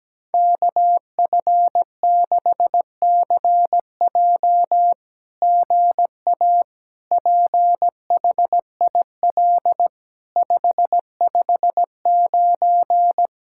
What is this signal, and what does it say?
Morse code, heard as a keyed tone, about 17 wpm, 700 Hz. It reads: KF6CJ GA PHIL 559
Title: SOTA QSO Practice 17wpm